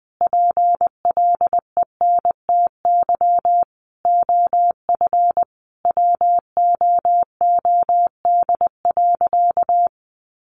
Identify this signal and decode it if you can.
Morse code, heard as a keyed tone, about 19 wpm, 700 Hz. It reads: PLENTY OF WOOD.
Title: Sentences from Top 1300 Words 20wpm